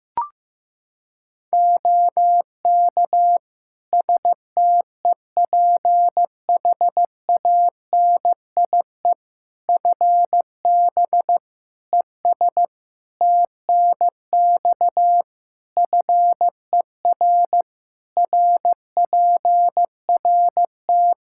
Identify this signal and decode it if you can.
Morse code, heard as a keyed tone, about 16 wpm, 700 Hz. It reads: OK STEPHANIE FB ES TNX FER RPRT
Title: Practice QSO - Exchange 2 - Info - 15wpm